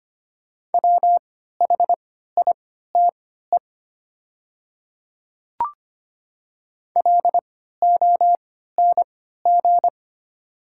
Morse code, heard as a keyed tone, about 25 wpm, spaced as for 14 wpm, 700 Hz. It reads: WHITE LONG
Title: 3x Character Spacing - Top 300 Words - 25wpm